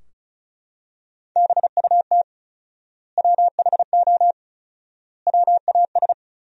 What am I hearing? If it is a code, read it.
Morse code, heard as a keyed tone, about 36 wpm, 700 Hz. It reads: BUT WHO WAS
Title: 4x Word-Spacing - Sentences from Top 100 Words - 35wpm